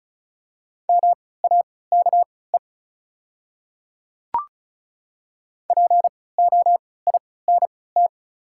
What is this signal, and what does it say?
Morse code, heard as a keyed tone, about 36 wpm, spaced as for 20 wpm, 700 Hz. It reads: MAKE POINT
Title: 3x Character Spacing - Top 200 Words - 35wpm